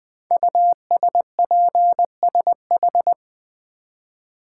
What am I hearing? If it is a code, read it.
Morse code, heard as a keyed tone, about 20 wpm, 700 Hz. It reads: USPSH
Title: Code Groups - 5 Letters - 20wpm